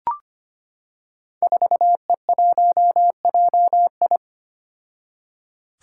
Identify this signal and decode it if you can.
Morse code, heard as a keyed tone, about 25 wpm, 700 Hz. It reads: 4E1JI
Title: Five Challenging Letter-Number Combinations 25wpm